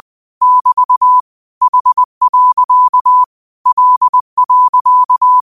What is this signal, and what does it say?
Morse code, heard as a keyed tone, about 20 wpm, 1000 Hz. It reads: <BT> H. L.
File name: qotdfiles / 20wpm